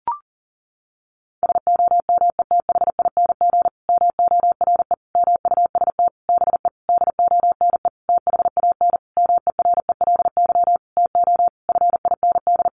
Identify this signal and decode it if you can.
Morse code, heard as a keyed tone, about 40 wpm, 700 Hz. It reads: SOMETHING MORE MUST BE DONE THAN MERELY TO FIND
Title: Sentences from Top 1400 Words 40wpm